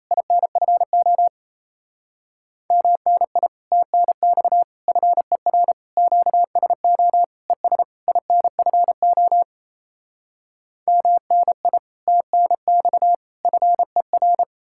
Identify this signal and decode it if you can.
Morse code, heard as a keyed tone, about 37 wpm, 700 Hz. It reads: INFO MNI TNX FER QSO ES INFO MNI TNX FER